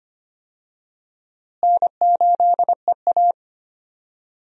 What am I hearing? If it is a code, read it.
Morse code, heard as a keyed tone, about 25 wpm, 700 Hz. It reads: N8EA
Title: USA Call Signs - 4 Characters - Repeated 3x - 25wpm